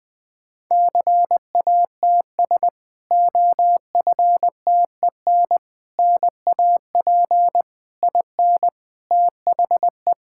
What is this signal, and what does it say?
Morse code, heard as a keyed tone, about 21 wpm, 700 Hz. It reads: CATS OFTEN NAP IN THE
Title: Difficult Sentences with Hints - 20wpm